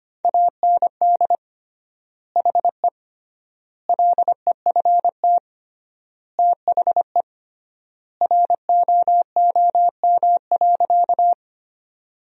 Morse code, heard as a keyed tone, about 25 wpm, 700 Hz. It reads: AND HE LEFT THE ROOM.
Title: 3x Word-Spacing - Sentences from Top 300 Words - 25wpm